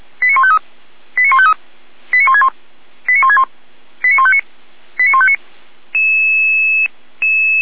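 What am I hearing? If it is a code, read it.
Five-tone selective call (ZVEI), heard as quick runs of five tones, tones of about 70 ms. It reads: 89153 89153 89161 89161 89159 89159
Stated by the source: ZVEI five-tone selective call 89160 received on an SDR.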